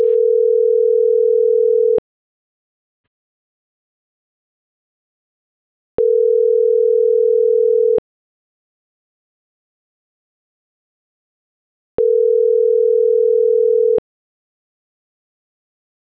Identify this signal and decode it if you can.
Ringback tone (North America), 440 + 480 Hz.